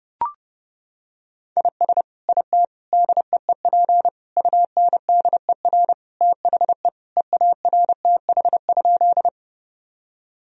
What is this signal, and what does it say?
Morse code, heard as a keyed tone, about 30 wpm, 700 Hz. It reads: IS IT DEEP UNDER THE EARTH?
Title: Sentences from Top 1000 Words - Review Part 3 - 30wpm